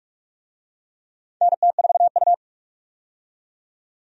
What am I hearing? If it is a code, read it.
Morse code, heard as a keyed tone, about 46 wpm, 700 Hz. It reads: NT4U
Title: USA Call Signs - 4 Characters - 45wpm